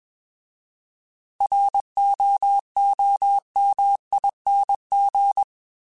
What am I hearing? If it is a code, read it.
Morse code, heard as a keyed tone, about 21 wpm, 800 Hz. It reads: ROOMING